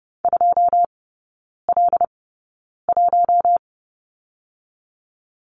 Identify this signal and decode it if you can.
Morse code, heard as a keyed tone, about 30 wpm, spaced as for 9 wpm, 690 Hz. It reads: 2L1
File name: Lesson 17e - Letter u - Three Characters 30wpm